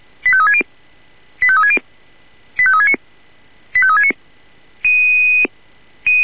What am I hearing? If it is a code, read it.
Five-tone selective call (ZVEI), heard as quick runs of five tones, tones of about 70 ms. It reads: 85360 85360 85369 85369